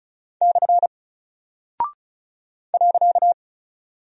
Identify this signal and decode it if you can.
Morse code, heard as a keyed tone, about 36 wpm, 700 Hz. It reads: / .
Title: Single Character (Letter:Number:Punctuation:Prosign) Alt 35wpm